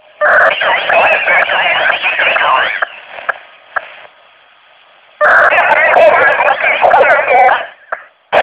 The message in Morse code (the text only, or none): none